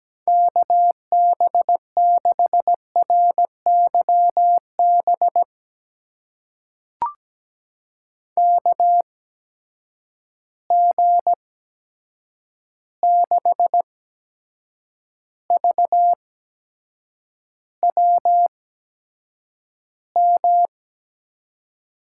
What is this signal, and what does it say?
Morse code, heard as a keyed tone, about 17 wpm, spaced as for 7 wpm, 700 Hz. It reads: KB6RYB KG6VWM
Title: USA Call Signs - 6 Characters - 8x Character Spacing - 17wpm